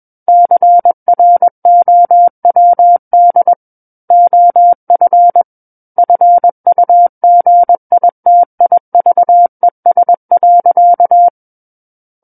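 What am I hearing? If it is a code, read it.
Morse code, heard as a keyed tone, about 21 wpm, 700 Hz. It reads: CROWD OF FUGITIVES.
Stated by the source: Morse code practice reading of a book chapter at 21 words per minute.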